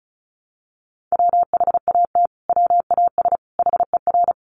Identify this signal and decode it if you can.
Morse code, heard as a keyed tone, about 34 wpm, 700 Hz. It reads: WHAT WAS HER
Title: Sentences from Top 1400 Words 35wpm